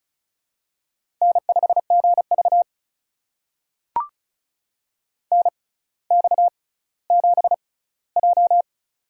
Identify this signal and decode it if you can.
Morse code, heard as a keyed tone, about 34 wpm, spaced as for 18 wpm, 700 Hz. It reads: N5GV NX7J
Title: USA Call Signs - 6x Character Spacing - 35wpm